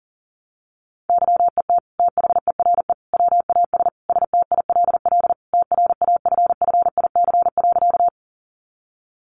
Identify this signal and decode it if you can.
Morse code, heard as a keyed tone, about 40 wpm, 700 Hz. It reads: YET THERE WAS STILL TRAFFIC.
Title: Sentences from Top 1300 Words 40wpm